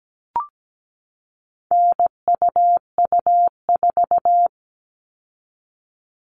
Morse code, heard as a keyed tone, about 17 wpm, 700 Hz. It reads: NUU4
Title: All Cut Numbers - 4-Digit 17wpm